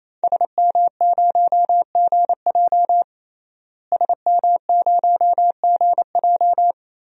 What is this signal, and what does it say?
Morse code, heard as a keyed tone, about 28 wpm, 700 Hz. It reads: SM0GJ SM0GJ
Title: DXCC-100 Call Signs - 5 Characters - Repeated 3x - 28wpm